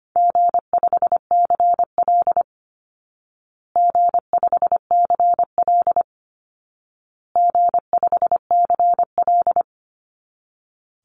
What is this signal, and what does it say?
Morse code, heard as a keyed tone, about 25 wpm, 700 Hz. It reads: G5CL G5CL G5CL
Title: Easy Call Signs Repeated 3x - 25wpm